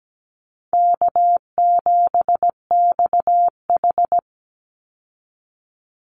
Morse code, heard as a keyed tone, about 17 wpm, 700 Hz. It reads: K7XH
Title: Easy Call Signs Repeated 3x - 17wpm